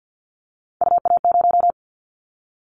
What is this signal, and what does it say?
Morse code, heard as a keyed tone, about 50 wpm, 700 Hz. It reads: UA0
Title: Callsign Prefixes for Top 25 DXCC - Repeated 3x - 50wpm